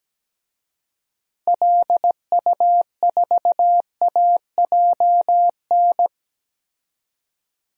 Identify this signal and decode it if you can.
Morse code, heard as a keyed tone, about 17 wpm, 700 Hz. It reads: LU4AJN
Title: DXCC-100 Call Signs - 6 Characters - Repeated 2x - 17wpm